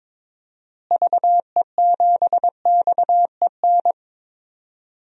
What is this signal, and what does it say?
Morse code, heard as a keyed tone, about 22 wpm, 700 Hz. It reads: VE7XEN
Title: DXCC-100 Call Signs - 6 Characters - Repeated 2x - 22wpm